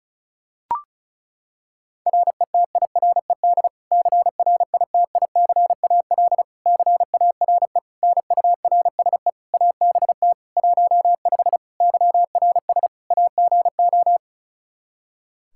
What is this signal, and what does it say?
Morse code, heard as a keyed tone, about 36 wpm, 700 Hz. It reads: RETIRED CRITICAL CARE NURSE ABT 15 YRS AGO
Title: Practice QSO - Exchange 3 - More Info - 35wpm